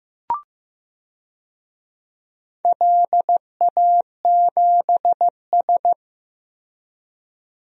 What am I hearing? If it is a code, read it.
Morse code, heard as a keyed tone, about 15 wpm, 700 Hz. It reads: LA7S
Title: DXCC-100 Call Signs - 4 Characters - Repeated 3x - 15wpm